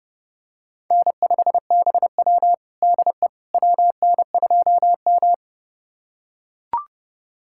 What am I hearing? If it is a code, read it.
Morse code, heard as a keyed tone, about 29 wpm, 700 Hz. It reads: N5BW DE WN2M
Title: USA Callsign DE Callsign - 4 Characters - 30wpm